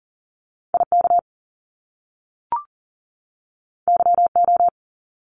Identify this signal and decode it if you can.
Morse code, heard as a keyed tone, about 40 wpm, 700 Hz. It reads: IK YO
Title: Top Two Letter Combinations in Words 40wpm